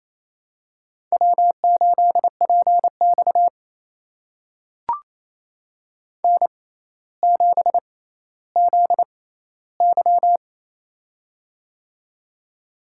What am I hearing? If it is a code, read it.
Morse code, heard as a keyed tone, about 28 wpm, spaced as for 14 wpm, 700 Hz. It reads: W8PX N7ZY
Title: USA Call Signs - 4 Characters - 6x Character Spacing - 28wpm